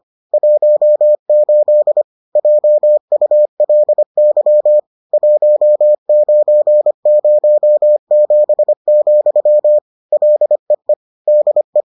Morse code, heard as a keyed tone, about 25 wpm, 600 Hz. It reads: 18 JULY 1907, LEE DE